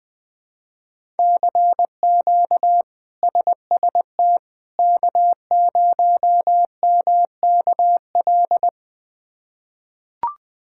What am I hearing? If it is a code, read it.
Morse code, heard as a keyed tone, about 21 wpm, 700 Hz. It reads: CQ SST K0MKL
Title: K1USN SST QSO Practice - 20wpm